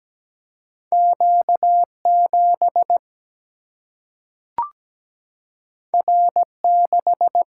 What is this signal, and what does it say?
Morse code, heard as a keyed tone, about 17 wpm, 700 Hz. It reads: Q7 R6
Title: Single-Letter Single-Number 17wpm